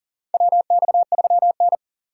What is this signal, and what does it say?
Morse code, heard as a keyed tone, about 37 wpm, 700 Hz. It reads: WX3N
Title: USA Call Signs - 4 Characters - 3x Character Spacing - 40wpm